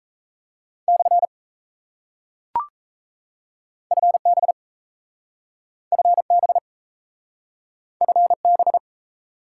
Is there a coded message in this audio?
Yes: Morse code from 0.9 to 8.8 s, heard as a keyed tone, about 37 wpm, 700 Hz. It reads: / FB FB FB